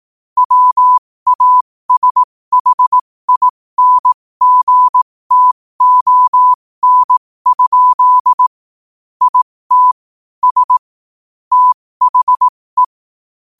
Morse code, heard as a keyed tone, about 18 wpm, spaced as for 15 wpm, 1000 Hz. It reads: WASHINGTON? IT S THE
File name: qotdfiles / 15wpm